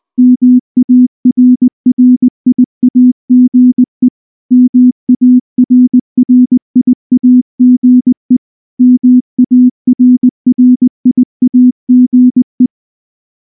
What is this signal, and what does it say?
Morse code, heard as a keyed tone, about 20 wpm, 250 Hz. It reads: MARRIAGE MARRIAGE MARRIAGE
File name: morse-training-20WPM